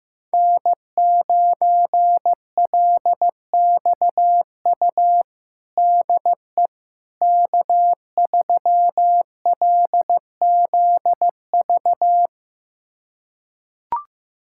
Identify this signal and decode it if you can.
Morse code, heard as a keyed tone, about 15 wpm, 700 Hz. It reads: N9LXU DE K3LZV
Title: USA Callsign DE Callsign - 5 Characters - 15wpm